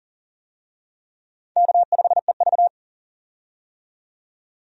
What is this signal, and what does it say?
Morse code, heard as a keyed tone, about 40 wpm, 700 Hz. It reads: K5EV
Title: USA Call Signs - 4 Characters - 40wpm